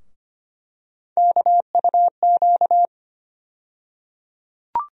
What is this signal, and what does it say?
Morse code, heard as a keyed tone, about 25 wpm, 700 Hz. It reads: KUQ
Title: USA Call Sign Postfixes - 3 Letters 25wpm